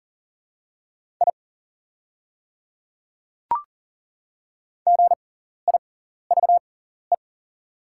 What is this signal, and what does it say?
Morse code, heard as a keyed tone, about 40 wpm, spaced as for 14 wpm, 690 Hz. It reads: I GIVE
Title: 6x Character Spacing - Top 200 Words - 40wpm